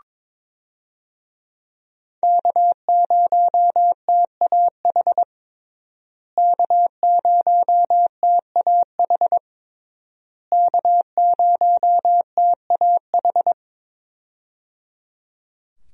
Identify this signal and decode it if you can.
Morse code, heard as a keyed tone, about 21 wpm, 700 Hz. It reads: K0TAH K0TAH K0TAH
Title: USA Call Signs - 5 Characters - Repeated 3x - 22wpm